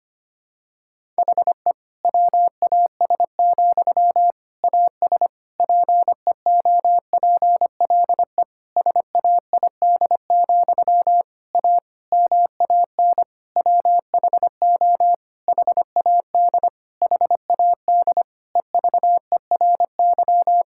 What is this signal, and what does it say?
Morse code, heard as a keyed tone, about 25 wpm, 700 Hz. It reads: HE WAS, AS PEOPLE SAID, A MAN WHO HAD HAD EVERY
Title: Sentences from Top 1000 Words - Review Part 6 - 25wpm